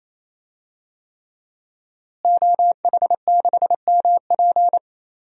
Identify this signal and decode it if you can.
Morse code, heard as a keyed tone, about 27 wpm, 700 Hz. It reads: OH6MP
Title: DXCC-100 Call Signs - 5 Characters - Repeated 3x - 28wpm